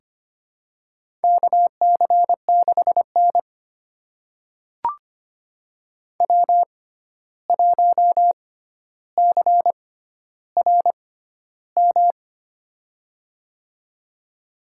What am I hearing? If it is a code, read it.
Morse code, heard as a keyed tone, about 25 wpm, spaced as for 12 wpm, 700 Hz. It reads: KC6N W1CRM
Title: USA Call Signs - 6x Character Spacing - 25wpm